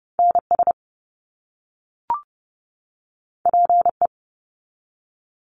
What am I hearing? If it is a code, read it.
Morse code, heard as a keyed tone, about 30 wpm, 700 Hz. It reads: NS PE
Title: Canadian Provinces and Territories 30wpm